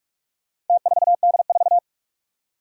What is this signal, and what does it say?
Morse code, heard as a keyed tone, about 44 wpm, 700 Hz. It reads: T4D4
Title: Four Challenging Letter-Number Combinations 45wpm